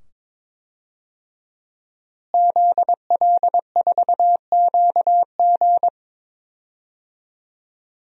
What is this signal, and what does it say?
Morse code, heard as a keyed tone, about 21 wpm, 700 Hz. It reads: ZL4QG
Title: DXCC-100 Call Signs - 5 Characters - Repeated 2x - 22wpm